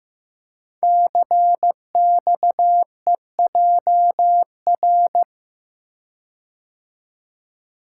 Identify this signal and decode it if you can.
Morse code, heard as a keyed tone, about 15 wpm, 700 Hz. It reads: CXEJR
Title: Code Groups - 5 Letter:Numbers - 15wpm